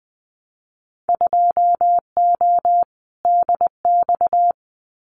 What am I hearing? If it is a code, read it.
Morse code, heard as a keyed tone, about 20 wpm, 700 Hz. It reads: 2O DX
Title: Winter Field Day Exchanges Encore 20wpm